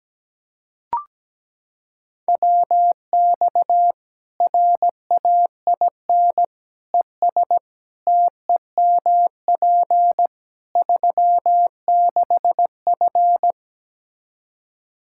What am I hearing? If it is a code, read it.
Morse code, heard as a keyed tone, about 17 wpm, 700 Hz. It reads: WX RAIN ES TEMP 36F
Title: Practice QSO - Exchange 2 - Info - 17wpm